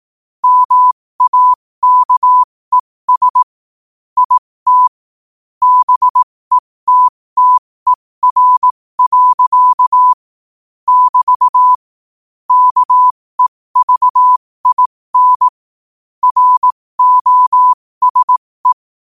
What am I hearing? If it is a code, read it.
Morse code, heard as a keyed tone, about 18 wpm, spaced as for 15 wpm, 1000 Hz. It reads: MAKES IT BETTER. <BT> KEVIN ROSE
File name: qotdfiles / 15wpm